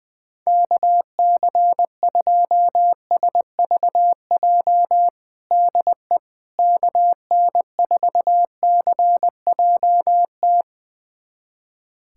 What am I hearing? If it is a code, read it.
Morse code, heard as a keyed tone, about 20 wpm, 700 Hz. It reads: KC2SVJ DE KN4CJT
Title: USA Callsign DE Callsign - 6 Characters - 20wpm